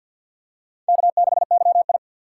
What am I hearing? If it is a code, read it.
Morse code, heard as a keyed tone, about 50 wpm, 700 Hz. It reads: K6YI
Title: USA Call Signs - 4 Characters - 6x Character Spacing - 50wpm